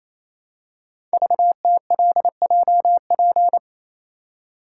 Morse code, heard as a keyed tone, about 29 wpm, 690 Hz. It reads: VTLJP
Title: Code Groups - 5 Letters - 28wpm